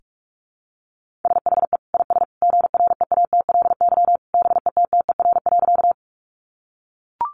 Morse code, heard as a keyed tone, about 44 wpm, 700 Hz. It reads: SHE IS GREATLY BETTER.
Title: Sentences from Top 1400 Words 45wpm